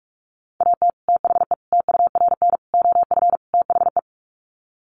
Morse code, heard as a keyed tone, about 44 wpm, 700 Hz. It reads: AT THE TURN OF THE
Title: Sets of 5 Words 45wpm